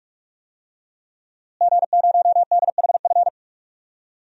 Morse code, heard as a keyed tone, about 46 wpm, 700 Hz. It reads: G0DHF
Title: DXCC-100 Call Signs - 5 Characters - Repeated 2x - 45wpm